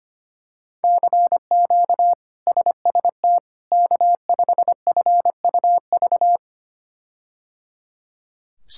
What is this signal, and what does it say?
Morse code, heard as a keyed tone, about 25 wpm, 700 Hz. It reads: CQ SST K5FUV
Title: K1USN SST QSO Practice - 25wpm